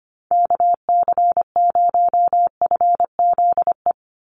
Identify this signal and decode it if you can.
Morse code, heard as a keyed tone, about 25 wpm, 700 Hz. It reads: KC0FZE
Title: USA Call Signs 25wpm